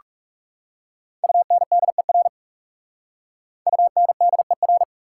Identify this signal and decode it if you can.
Morse code, heard as a keyed tone, about 44 wpm, 690 Hz. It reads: UNDER UNDER